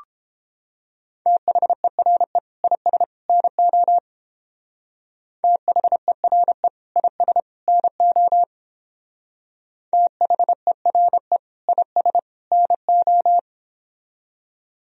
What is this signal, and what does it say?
Morse code, heard as a keyed tone, about 29 wpm, 700 Hz. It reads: THERE IS NO THERE IS NO THERE IS NO